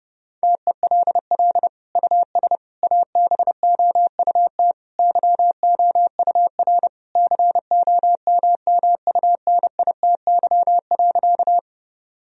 Morse code, heard as a keyed tone, about 30 wpm, 700 Hz. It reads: TELL US ABOUT YOUR COMMUNITY.
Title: Sentences from Top 1000 Words - Review Part 4 - 30wpm